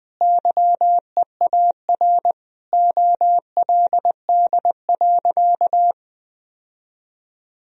Morse code, heard as a keyed tone, about 20 wpm, 700 Hz. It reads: YEAR OLD.
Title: Sentences from Top 1000 Words - Review Part 1 - 20wpm